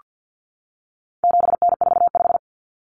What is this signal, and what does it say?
Morse code, heard as a keyed tone, about 48 wpm, 700 Hz. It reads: 7N45
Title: Common Cut Numbers - 4-Digit 50wpm